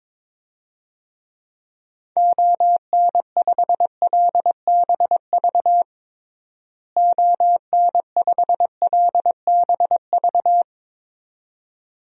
Morse code, heard as a keyed tone, about 22 wpm, 700 Hz. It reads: ON5LBV ON5LBV
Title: DXCC-100 Call Signs - 6 Characters - Repeated 2x - 22wpm